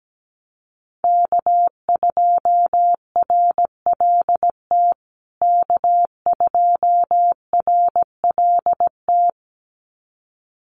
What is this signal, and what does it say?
Morse code, heard as a keyed tone, about 17 wpm, 700 Hz. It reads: K2RLT K2RLT
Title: USA Call Signs Repeated 2x 17wpm